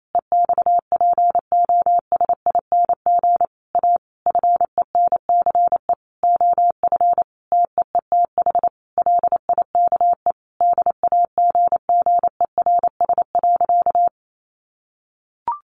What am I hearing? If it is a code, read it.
Morse code, heard as a keyed tone, about 28 wpm, 700 Hz. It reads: EXPOSING A FENCE OF TEETH LIKE DAGGERS.
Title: A Sound Of Thunder 28wpm